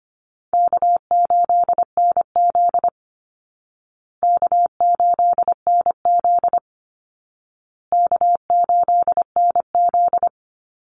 Morse code, heard as a keyed tone, about 25 wpm, 700 Hz. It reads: K8NZ K8NZ K8NZ
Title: Easy Call Signs Repeated 3x - 25wpm